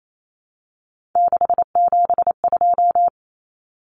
Morse code, heard as a keyed tone, about 28 wpm, 700 Hz. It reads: 672
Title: Numbers - 3-Digit 28wpm